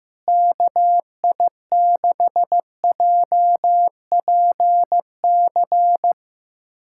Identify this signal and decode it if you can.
Morse code, heard as a keyed tone, about 15 wpm, 700 Hz. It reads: KI6JPC
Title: USA Callsign DE Callsign 15wpm